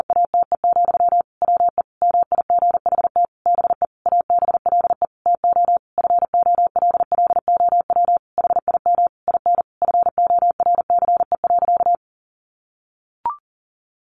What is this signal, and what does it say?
Morse code, heard as a keyed tone, about 40 wpm, 700 Hz. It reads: ATE, WE MIGHT BE ABLE TO FOLLOW HIM IN FORCE.
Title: Sentences from Top 700 Words 40wpm